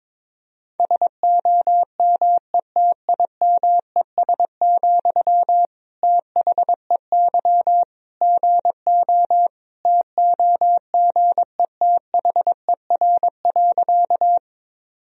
Morse code, heard as a keyed tone, about 22 wpm, 700 Hz. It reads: SOMETIMES, THEY GO TOGETHER.
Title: Sentences from Top 1000 Words - Review Part 3 - 22wpm